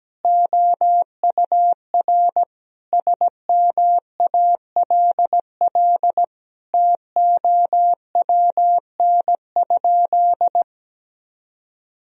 Morse code, heard as a keyed tone, about 17 wpm, 700 Hz. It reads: OUR SMALL TOWN?
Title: Sentences from Top 1000 Words - Review Part 4 - 17wpm